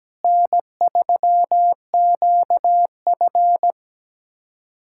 Morse code, heard as a keyed tone, about 17 wpm, 700 Hz. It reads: N3QF
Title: USA Callsign DE Callsign - 4 Characters - 17wpm